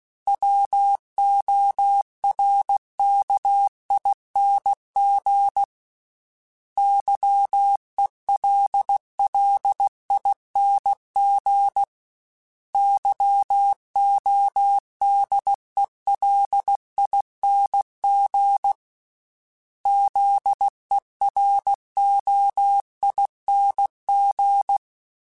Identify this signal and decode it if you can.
Morse code, heard as a keyed tone, about 16 wpm, 800 Hz. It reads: WORKING YELLING YODELING ZEROING
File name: Special INGs / Special INGs-mp3s / Sngl ing @15